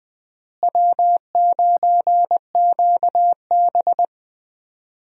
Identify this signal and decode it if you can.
Morse code, heard as a keyed tone, about 21 wpm, 700 Hz. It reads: W9QB
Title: USA Call Signs - 4 Characters - Repeated 2x - 20wpm